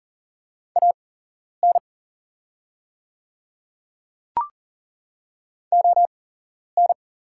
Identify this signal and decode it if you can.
Morse code, heard as a keyed tone, about 40 wpm, spaced as for 11 wpm, 700 Hz. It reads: AN ON
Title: 8x Character Spacing - Top 500 Words - 40wpm